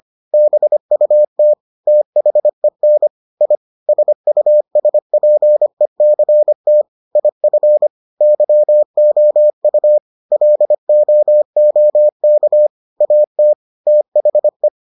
Morse code, heard as a keyed tone, about 25 wpm, 600 Hz. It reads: BUT THEN I SUSPECT IF YOU LOOK AT THE